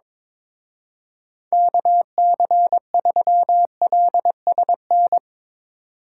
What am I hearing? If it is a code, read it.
Morse code, heard as a keyed tone, about 24 wpm, 700 Hz. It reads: KC3LSN
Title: Speed Racing - USA 2x3 Call Signs 22wpm